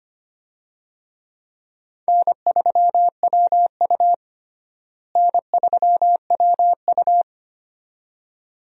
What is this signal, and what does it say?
Morse code, heard as a keyed tone, about 25 wpm, 700 Hz. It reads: N3WU N3WU
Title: USA Call Signs - 4 Characters - Repeated 2x - 25wpm